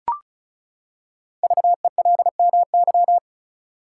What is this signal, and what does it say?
Morse code, heard as a keyed tone, about 36 wpm, 700 Hz. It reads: VELMY
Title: Code Groups - 5 Letters - 35wpm